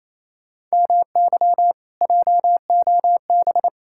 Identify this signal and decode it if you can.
Morse code, heard as a keyed tone, about 38 wpm, spaced as for 33 wpm, 700 Hz. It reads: MY JOB
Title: Speed-Racing Common QSO Phrases - 28wpm